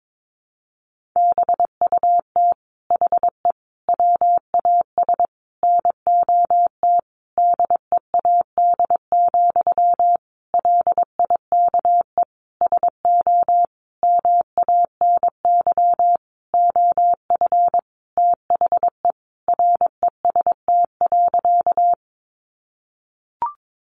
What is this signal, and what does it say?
Morse code, heard as a keyed tone, about 22 wpm, 700 Hz. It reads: BUT HE WAS NOT DEAD, LIKE SO MANY OF THE REST.
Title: Sentences from Top 1500 Words 22wpm